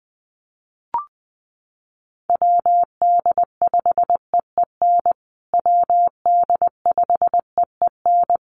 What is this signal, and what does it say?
Morse code, heard as a keyed tone, about 20 wpm, 700 Hz. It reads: WD5EEN WD5EEN
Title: USA Call Signs Repeated 3x 20wpm